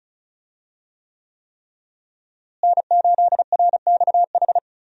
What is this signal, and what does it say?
Morse code, heard as a keyed tone, about 36 wpm, 700 Hz. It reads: N8RXH
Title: USA Call Signs - 5 Characters - Repeated 3x - 35wpm